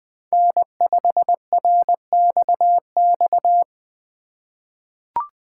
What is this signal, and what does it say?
Morse code, heard as a keyed tone, about 21 wpm, 700 Hz. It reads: N5RXX
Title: USA Call Signs - 2x Character Spacing - 20wpm